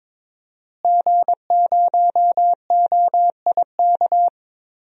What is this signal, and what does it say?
Morse code, heard as a keyed tone, about 22 wpm, 700 Hz. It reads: G0OIK
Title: DXCC-100 Call Signs - 5 Characters - Repeated 2x - 22wpm